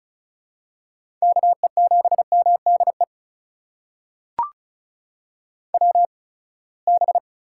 Morse code, heard as a keyed tone, about 36 wpm, spaced as for 24 wpm, 700 Hz. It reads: KE7MDE WB
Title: USA Call Signs - 6 Characters - 8x Character Spacing - 35wpm